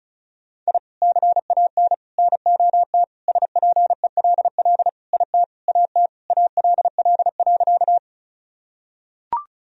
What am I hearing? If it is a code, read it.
Morse code, heard as a keyed tone, about 34 wpm, 700 Hz. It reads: I CAN NOT SPELL IT AT ALL.
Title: Sentences from Top 1000 Words - Review Part 1 - 35wpm